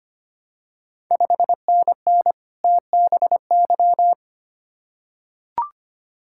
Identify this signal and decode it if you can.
Morse code, heard as a keyed tone, about 25 wpm, 700 Hz. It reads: 5NN TBY
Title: Ontario QSO Party Exchange 25wpm